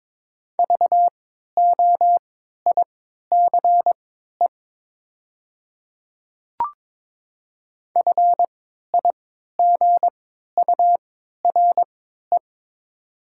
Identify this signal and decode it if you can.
Morse code, heard as a keyed tone, about 22 wpm, spaced as for 12 wpm, 700 Hz. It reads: VOICE FIGURE
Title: 3x Character Spacing - Top 500 Words - 22wpm